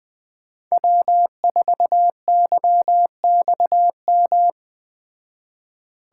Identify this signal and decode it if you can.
Morse code, heard as a keyed tone, about 20 wpm, 700 Hz. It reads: W4YXM
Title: Speed Racing - USA 1x3 Call Signs 17wpm